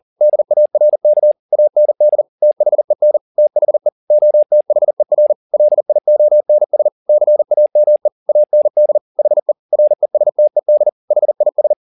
Morse code, heard as a keyed tone, about 40 wpm, 600 Hz. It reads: DARK AND THEN THE OTHER LIONS CAME AND HE RESTED HIS
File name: the old man and the sea 2 40wpm